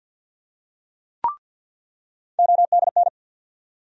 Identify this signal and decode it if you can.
Morse code, heard as a keyed tone, about 47 wpm, 700 Hz. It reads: ODN
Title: USA Call Sign Postfixes - 3 Letters 50wpm